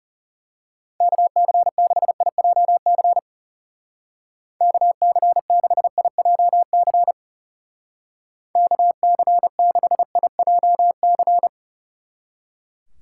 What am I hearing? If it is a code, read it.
Morse code, heard as a keyed tone, about 33 wpm, 700 Hz. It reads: KC6IJC KC6IJC KC6IJC
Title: Speed Racing - USA Call Signs 30wpm